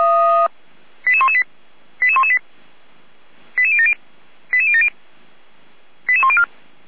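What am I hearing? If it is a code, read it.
Five-tone selective call (ZVEI), heard as quick runs of five tones, tones of about 80 ms. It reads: 88118 88118 88070 88070 88104
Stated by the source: ZVEI five-tone selective call 88192, received by SDR, recorded by the Orlamünde fire brigade (feuerwehr-orlamuende).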